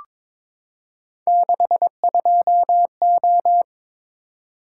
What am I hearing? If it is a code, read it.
Morse code, heard as a keyed tone, about 22 wpm, 700 Hz. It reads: 62O
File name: practice / 22wpm / Three Letter-Number Combinations - 22wpm